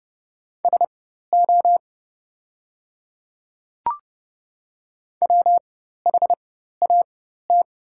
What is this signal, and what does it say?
Morse code, heard as a keyed tone, about 30 wpm, spaced as for 14 wpm, 690 Hz. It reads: SO WHAT
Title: 4x Character Spacing - Top 500 Words - 30wpm